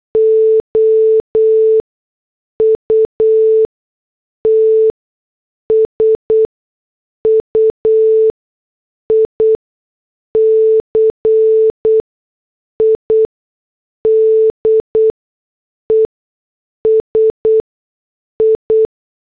Morse code, heard as a keyed tone, about 8 wpm, spaced as for 6 wpm, 440 Hz. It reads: OUT SUICIDES I
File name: morse / headlines